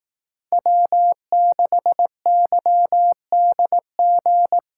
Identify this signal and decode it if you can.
Morse code, heard as a keyed tone, about 18 wpm, 700 Hz. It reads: W6YDG
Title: Speed Racing - USA 1x3 Call Signs 15wpm